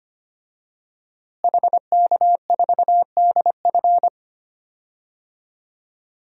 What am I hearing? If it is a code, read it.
Morse code, heard as a keyed tone, about 25 wpm, 700 Hz. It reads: HK4DF
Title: DXCC-100 Call Signs - 5 Characters - Repeated 2x - 25wpm